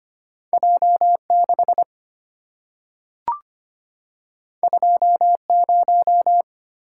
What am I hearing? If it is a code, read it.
Morse code, heard as a keyed tone, about 25 wpm, 700 Hz. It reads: J6 20
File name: Two Letter-Number Combinations - 25wpm